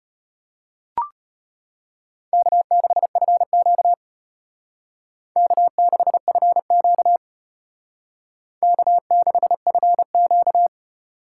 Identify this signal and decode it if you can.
Morse code, heard as a keyed tone, about 32 wpm, 700 Hz. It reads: K6FQ K6FQ K6FQ